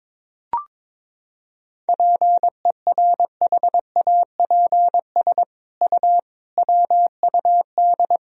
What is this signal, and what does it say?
Morse code, heard as a keyed tone, about 22 wpm, 700 Hz. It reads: PERHAPS U WUD
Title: Sentences from Top 500 Words and Abbreviations 22wpm